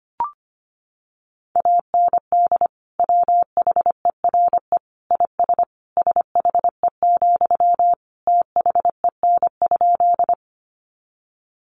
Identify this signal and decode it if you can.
Morse code, heard as a keyed tone, about 25 wpm, 700 Hz. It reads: AND WHERE IS SHE, THEN?
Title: Sentences from Top 200 Words 25wpm